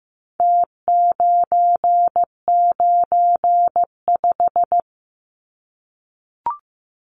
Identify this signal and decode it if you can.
Morse code, heard as a keyed tone, about 15 wpm, 700 Hz. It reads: T995
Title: Numbers - Contest Style - 4-Digit 15wpm